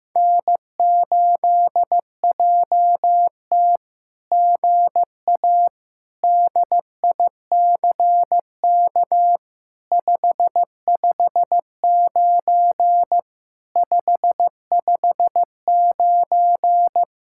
Morse code, heard as a keyed tone, about 15 wpm, 700 Hz. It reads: N8JT GA DICK 559 559
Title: SOTA QSO Practice 15wpm